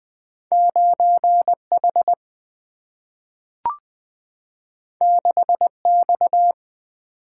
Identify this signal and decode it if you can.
Morse code, heard as a keyed tone, about 20 wpm, 700 Hz. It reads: 9 H 6 X
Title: Single-Letter Single-Number 20wpm